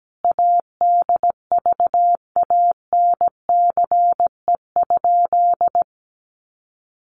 Morse code, heard as a keyed tone, about 17 wpm, 700 Hz. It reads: ADVANCE?
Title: Sentences from Top 1400 Words 17wpm